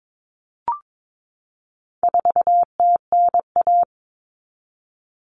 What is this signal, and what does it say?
Morse code, heard as a keyed tone, about 22 wpm, 700 Hz. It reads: 4TNA
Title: Common Cut Numbers - 4-Digit 22wpm